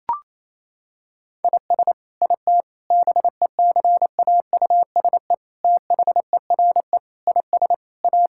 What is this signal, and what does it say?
Morse code, heard as a keyed tone, about 28 wpm, 700 Hz. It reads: IS IT BECAUSE THERE IS A
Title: Sentences from Top 1000 Words - Review Part 5 - 28wpm